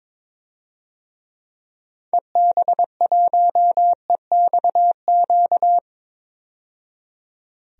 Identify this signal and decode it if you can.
Morse code, heard as a keyed tone, about 22 wpm, 700 Hz. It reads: EB1EXQ
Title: DXCC-100 Call Signs - 6 Characters - 22wpm